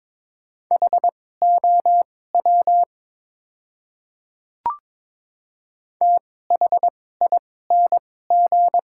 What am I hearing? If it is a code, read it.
Morse code, heard as a keyed tone, about 22 wpm, spaced as for 16 wpm, 700 Hz. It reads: HOW THING
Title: 2x Character Spacing - Top 100 Words - 22wpm